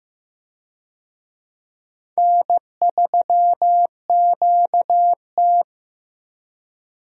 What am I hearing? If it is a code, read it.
Morse code, heard as a keyed tone, about 16 wpm, 700 Hz. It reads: N3QT
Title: USA Call Signs - 4 Characters - Repeated 3x - 15wpm